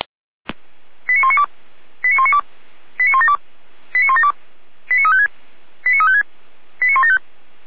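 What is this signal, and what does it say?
Five-tone selective call (ZVEI), heard as quick runs of five tones, tones of about 70 ms. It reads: 89192 89192 89172 89172 89356 89356 89175
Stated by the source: ZVEI five-tone call 89127 received with an SDR, recorded on the Feuerwehr Orlamünde server.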